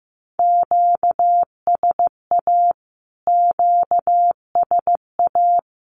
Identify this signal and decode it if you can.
Morse code, heard as a keyed tone, about 15 wpm, 700 Hz. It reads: QSA QSA
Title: Q-Codes Repeated 3x 15wpm